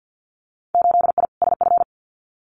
Morse code, heard as a keyed tone, about 50 wpm, 700 Hz. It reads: 8I SF
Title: Winter Field Day Exchanges Encore 50wpm